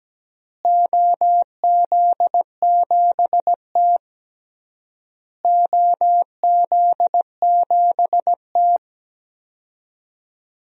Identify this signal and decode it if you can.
Morse code, heard as a keyed tone, about 17 wpm, 700 Hz. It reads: OZ7T OZ7T
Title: DXCC-100 Call Signs - 4 Characters - Repeated 3x - 17wpm